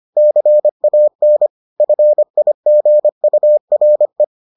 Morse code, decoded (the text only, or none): CAN FIGURE